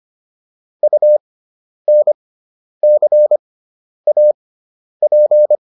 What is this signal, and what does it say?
Morse code, heard as a keyed tone, about 25 wpm, spaced as for 10 wpm, 600 Hz. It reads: UNCAP